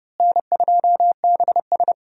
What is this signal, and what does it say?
Morse code, heard as a keyed tone, about 30 wpm, 700 Hz. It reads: N2BS
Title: USA Call Signs - 4 Characters - Repeated 3x - 30wpm